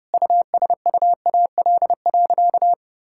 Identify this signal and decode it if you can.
Morse code, heard as a keyed tone, about 30 wpm, 700 Hz. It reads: USUAL.
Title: Sentences from Top 1000 Words - Review Part 5 - 30wpm